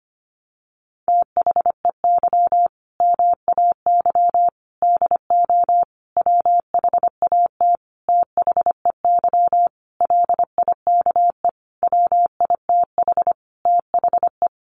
Morse code, heard as a keyed tone, about 25 wpm, 700 Hz. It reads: THEY MAY DO WHAT THEY LIKE WITH THE
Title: Sentences from Top 1200 Words 25wpm